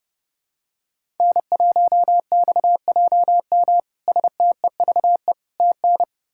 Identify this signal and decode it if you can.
Morse code, heard as a keyed tone, about 30 wpm, 700 Hz. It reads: N1XJM STEVE TN
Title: K1USN SST QSO Practice - 30wpm